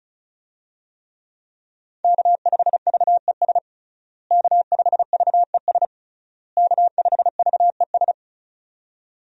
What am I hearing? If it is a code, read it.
Morse code, heard as a keyed tone, about 34 wpm, 700 Hz. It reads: K5VES K5VES K5VES
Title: USA Call Signs - 5 Characters - Repeated 3x - 35wpm